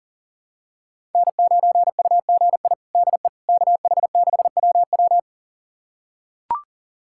Morse code, heard as a keyed tone, about 40 wpm, 700 Hz. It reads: N9UGI DE KH6WW
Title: USA Callsign DE Callsign - 5 Characters - 40wpm